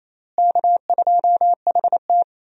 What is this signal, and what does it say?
Morse code, heard as a keyed tone, about 28 wpm, 700 Hz. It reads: K2HT
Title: USA Call Signs - 4 Characters - Repeated 3x - 28wpm